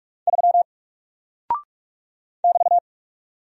Morse code, heard as a keyed tone, about 46 wpm, 700 Hz. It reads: 3 <BT>
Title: Single Character (Letter:Number:Punctuation:Prosign) Alt 45wpm